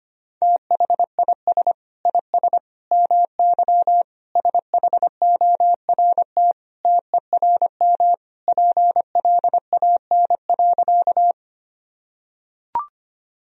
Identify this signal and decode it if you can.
Morse code, heard as a keyed tone, about 25 wpm, 700 Hz. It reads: THIS IS MY SHORT TERM PLAN.
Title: Sentences from Top 1000 Words - Review Part 4 - 25wpm